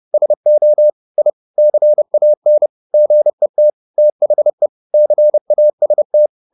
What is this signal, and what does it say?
Morse code, heard as a keyed tone, about 30 wpm, 600 Hz. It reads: SO I CAN GET THE CAST
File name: the old man and the sea / the old man and the sea 1 30wpm